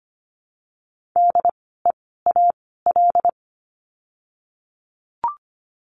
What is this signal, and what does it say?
Morse code, heard as a keyed tone, about 25 wpm, 700 Hz. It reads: D E A L
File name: course / 25:10wpm / 094 Lesson 16f - Letter c - Words 25wpm